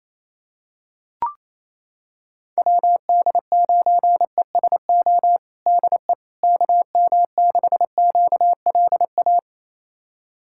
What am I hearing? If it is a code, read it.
Morse code, heard as a keyed tone, about 28 wpm, 700 Hz. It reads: WD9ESO DE KM6QLA
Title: USA Callsign DE Callsign 28wpm